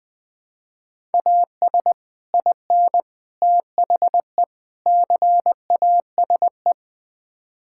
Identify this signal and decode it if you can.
Morse code, heard as a keyed tone, about 20 wpm, 700 Hz. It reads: AS IN THE CASE
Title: Build A Phrase - 5-Word Phrases - 20wpm